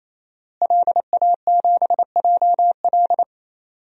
Morse code, heard as a keyed tone, about 27 wpm, 700 Hz. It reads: LA7JL
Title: DXCC-100 Call Signs - 5 Characters - Repeated 3x - 28wpm